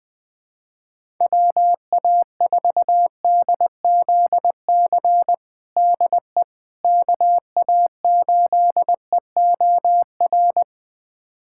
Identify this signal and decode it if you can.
Morse code, heard as a keyed tone, about 21 wpm, 700 Hz. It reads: WA4DZC DE KA8EOR
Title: USA Callsign DE Callsign - 6 Characters - 20wpm